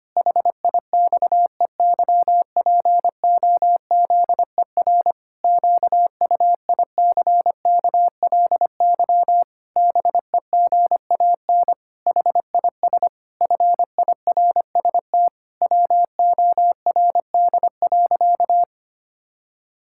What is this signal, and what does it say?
Morse code, heard as a keyed tone, about 25 wpm, 700 Hz. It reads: HIXEYPOZER QUICKLY BEGAN HIS FIRST WORD.
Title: Pangram Challenge v2 - 25wpm